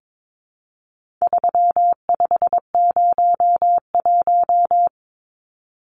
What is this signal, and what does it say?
Morse code, heard as a keyed tone, about 22 wpm, 700 Hz. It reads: 3501
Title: Numbers - 4-Digit 22wpm